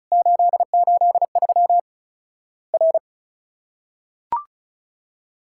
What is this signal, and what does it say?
Morse code, heard as a keyed tone, about 34 wpm, 700 Hz. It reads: 883 R
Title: Worked All Europe DX Contest - QTC Exchange - 35wpm